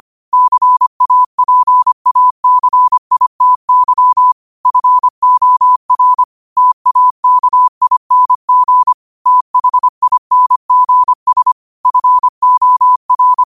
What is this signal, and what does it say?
Morse code, heard as a keyed tone, about 25 wpm, 1000 Hz. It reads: CAPACITY FOR TAKING THINGS FOR
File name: qotdfiles / 25wpm